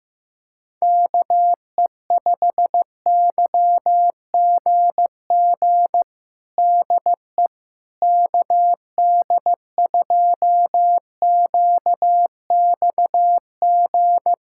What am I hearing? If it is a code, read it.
Morse code, heard as a keyed tone, about 16 wpm, 700 Hz. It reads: KE5YGG DE KD2QXG
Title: USA Callsign DE Callsign - 6 Characters - 15wpm